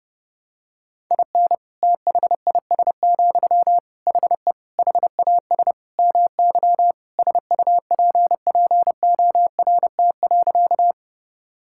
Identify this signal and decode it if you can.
Morse code, heard as a keyed tone, about 30 wpm, 700 Hz. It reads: IN THIS, HE HAS MY SUPPORT.
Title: Sentences from Top 1000 Words - Review Part 5 - 30wpm